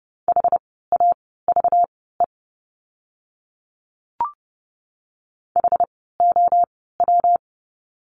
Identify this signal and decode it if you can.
Morse code, heard as a keyed tone, about 30 wpm, spaced as for 17 wpm, 700 Hz. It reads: HAVE HOW
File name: course / 30:10wpm / 230 Lesson 37f - Letter x - Words 30wpm